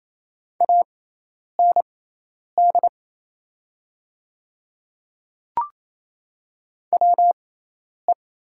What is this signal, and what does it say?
Morse code, heard as a keyed tone, about 28 wpm, spaced as for 10 wpm, 700 Hz. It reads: AND WE
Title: 6x Character Spacing - Top 300 Words - 28wpm